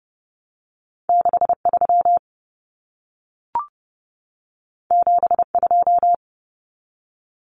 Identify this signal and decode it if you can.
Morse code, heard as a keyed tone, about 30 wpm, 700 Hz. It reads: 63 72
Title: Common Cut Numbers - 2-Digit 30wpm